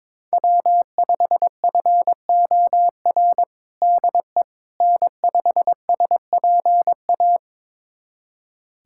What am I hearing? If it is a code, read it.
Morse code, heard as a keyed tone, about 22 wpm, 700 Hz. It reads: W5FOR DE N5SPA
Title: USA Callsign DE Callsign - 5 Characters - 22wpm